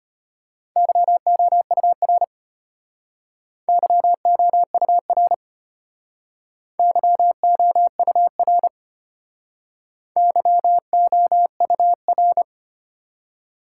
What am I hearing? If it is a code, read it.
Morse code, heard as a keyed tone, about 33 wpm, 700 Hz. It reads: YOUR YOUR YOUR YOUR